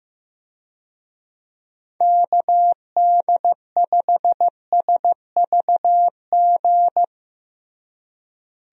Morse code, heard as a keyed tone, about 16 wpm, 700 Hz. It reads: KD5SVG
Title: USA Call Signs - 6 Characters - Repeated 2x - 15wpm